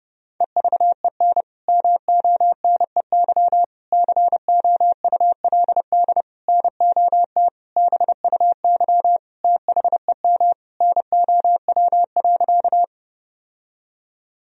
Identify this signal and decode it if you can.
Morse code, heard as a keyed tone, about 32 wpm, 700 Hz. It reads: EVEN MONEY COULD NOT BUY THEM NOW.
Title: Sentences from Top 1000 Words - Review Part 4 - 30wpm